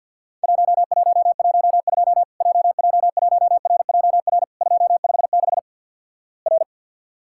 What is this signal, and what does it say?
Morse code, heard as a keyed tone, about 50 wpm, 700 Hz. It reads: 1112 JJ1RJR 256 R
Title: Worked All Europe DX Contest - QTC Exchange - 50wpm